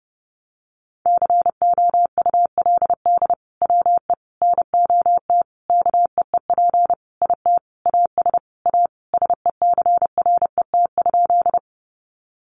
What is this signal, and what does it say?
Morse code, heard as a keyed tone, about 30 wpm, 700 Hz. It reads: COULD WE NOT KEEP IT AS A SECRET?
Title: Sentences from Top 1200 Words 30wpm